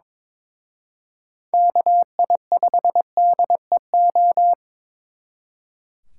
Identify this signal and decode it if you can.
Morse code, heard as a keyed tone, about 26 wpm, 700 Hz. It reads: KI5DEO
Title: Speed Racing - USA 2x3 Call Signs 22wpm